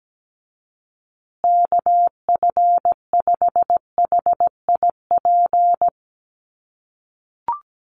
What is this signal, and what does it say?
Morse code, heard as a keyed tone, about 17 wpm, 700 Hz. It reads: KF5HIP
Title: USA Call Signs Repeated 3x 17wpm